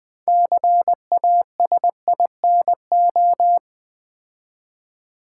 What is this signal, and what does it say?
Morse code, heard as a keyed tone, about 20 wpm, 700 Hz. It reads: CASINO
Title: 6 Letter Words - First 3 Letters Given - 20wpm